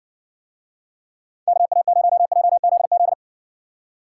Morse code, heard as a keyed tone, about 60 wpm, 700 Hz. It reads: KA0JZZ